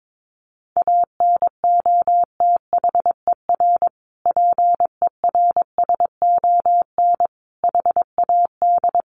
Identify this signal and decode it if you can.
Morse code, heard as a keyed tone, about 22 wpm, 700 Hz. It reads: ANOTHER PERSON HAD
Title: Sentences from Top 1400 Words 22wpm